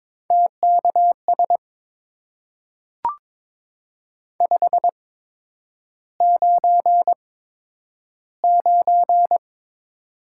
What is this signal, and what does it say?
Morse code, heard as a keyed tone, about 22 wpm, spaced as for 9 wpm, 700 Hz. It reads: TKS 599
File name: Top 90 QSO Elements - 8x Character Spacing - 22wpm